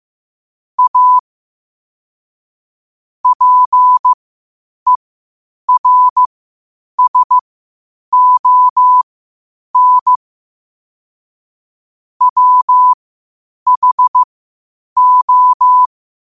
Morse code, heard as a keyed tone, about 15 wpm, spaced as for 8 wpm, 1000 Hz. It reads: A PERSON WHO
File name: qotdfiles / 8wpm